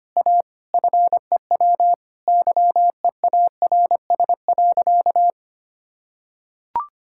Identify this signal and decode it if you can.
Morse code, heard as a keyed tone, about 25 wpm, 700 Hz. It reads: A FEW YEARS.
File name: practice / 25wpm / Sentences with 3-Word Phrases - 25wpm